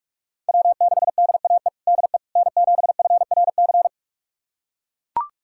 Morse code, heard as a keyed tone, about 42 wpm, 700 Hz. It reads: W6DAE DE N7FRC
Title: USA Callsign DE Callsign - 5 Characters - 45wpm